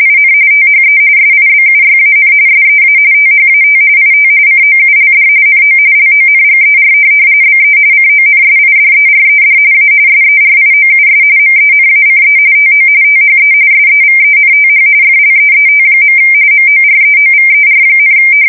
RTTY, about 45.45 baud. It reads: IN THE SIBUYAN SEA THAT IT COULD NO LONGER BE CONSIDERED A SERIOUS MENACE TO THE SEVENTH FLEET. MOREOVER,
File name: fsk170 / USS-New-Jersey